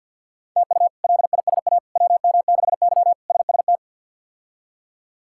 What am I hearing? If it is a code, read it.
Morse code, heard as a keyed tone, about 46 wpm, 700 Hz. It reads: TU LISA WM6Y SST
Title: K1USN SST QSO Practice - 50wpm